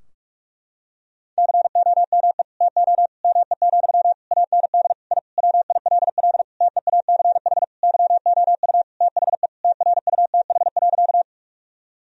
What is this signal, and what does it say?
Morse code, heard as a keyed tone, about 46 wpm, 700 Hz. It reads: COME TO ME, AND I WILL TEACH YOU THE TRUTH.
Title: Sentences from Top 1000 Words - Review Part 2 - 45wpm